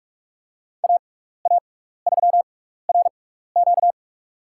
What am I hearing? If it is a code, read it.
Morse code, heard as a keyed tone, about 46 wpm, spaced as for 15 wpm, 700 Hz. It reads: AA3RQ
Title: USA Call Signs - 6x Character Spacing - 45wpm